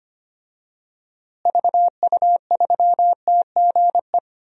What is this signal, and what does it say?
Morse code, heard as a keyed tone, about 25 wpm, 700 Hz. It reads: VU3TGE
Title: DXCC-100 Call Signs - 6 Characters - Repeated 2x - 25wpm